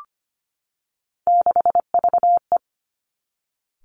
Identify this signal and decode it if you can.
Morse code, heard as a keyed tone, about 25 wpm, 700 Hz. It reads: 6VE
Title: Three Challenging Letter-Number Combinations 25wpm